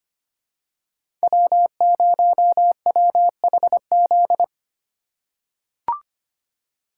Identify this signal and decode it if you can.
Morse code, heard as a keyed tone, about 25 wpm, 700 Hz. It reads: W0WHZ
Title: USA Call Signs - 5 Characters - 3x Character Spacing - 25wpm